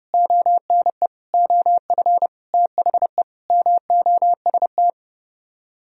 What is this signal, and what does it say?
Morse code, heard as a keyed tone, about 38 wpm, 700 Hz. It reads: ONE OF THE MOST
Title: Speed-Racing - Sets of 5 Words 30wpm